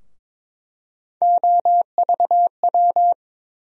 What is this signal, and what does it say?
Morse code, heard as a keyed tone, about 22 wpm, 700 Hz. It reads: OVW
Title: USA Call Sign Postfixes - 3 Letters 22wpm